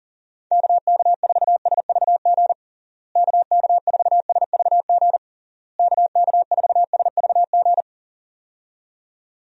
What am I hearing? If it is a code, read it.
Morse code, heard as a keyed tone, about 40 wpm, 700 Hz. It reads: KK4SVG KK4SVG KK4SVG
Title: USA Call Signs - 6 Characters - Repeated 3x - 40wpm